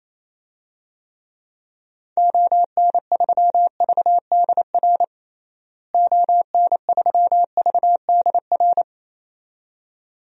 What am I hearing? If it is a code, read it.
Morse code, heard as a keyed tone, about 28 wpm, 700 Hz. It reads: ON3VDR ON3VDR
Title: DXCC-100 Call Signs - 6 Characters - Repeated 2x - 28wpm